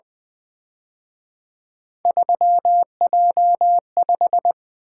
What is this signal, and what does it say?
Morse code, heard as a keyed tone, about 24 wpm, 700 Hz. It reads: 3J5
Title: Speed-Racing - Three Letter-Number Combinations - 20wpm